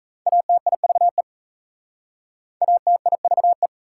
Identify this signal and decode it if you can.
Morse code, heard as a keyed tone, about 37 wpm, 700 Hz. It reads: ATIVE ATIVE